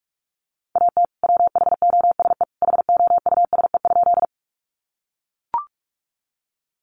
Morse code, heard as a keyed tone, about 44 wpm, 700 Hz. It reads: AT WHOSE HOUSE?
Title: Sentences from Top 1200 Words 45wpm